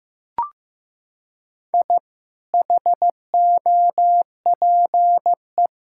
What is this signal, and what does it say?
Morse code, heard as a keyed tone, about 15 wpm, 700 Hz. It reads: I HOPE
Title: Sentences from Top 1000 Words - Review Part 6 - 15wpm